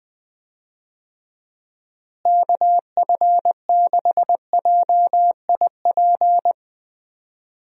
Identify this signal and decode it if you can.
Morse code, heard as a keyed tone, about 20 wpm, 700 Hz. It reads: KF6JIP
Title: USA Call Signs - 6 Characters - Repeated 3x - 20wpm